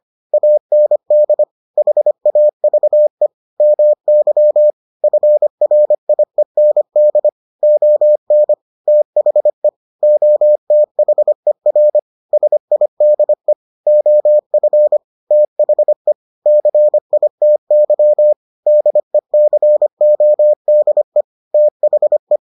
Morse code, heard as a keyed tone, about 25 wpm, 600 Hz. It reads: AND HAVE MY FRIEND ON THE OTHER SIDE OF THE CITY DECODE THE